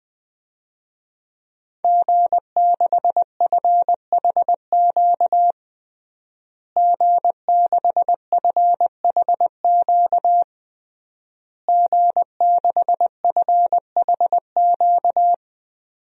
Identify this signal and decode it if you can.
Morse code, heard as a keyed tone, about 21 wpm, 700 Hz. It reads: G6FHQ G6FHQ G6FHQ
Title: DXCC-100 Call Signs - 5 Characters - Repeated 3x - 20wpm